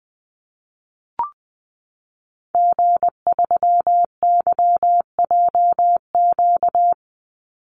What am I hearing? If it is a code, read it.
Morse code, heard as a keyed tone, about 20 wpm, 700 Hz. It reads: G3YJQ
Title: Call Signs - 20wpm